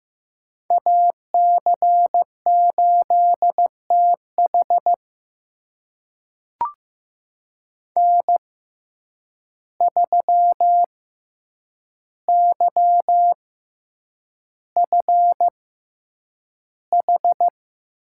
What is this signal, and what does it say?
Morse code, heard as a keyed tone, about 15 wpm, spaced as for 8 wpm, 700 Hz. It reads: AC8TH N3YFH
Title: USA Call Signs - 6x Character Spacing - 15wpm